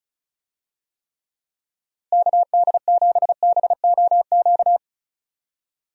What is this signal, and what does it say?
Morse code, heard as a keyed tone, about 36 wpm, 700 Hz. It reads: KD7BOQ
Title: USA Call Signs - 6 Characters - Repeated 3x - 35wpm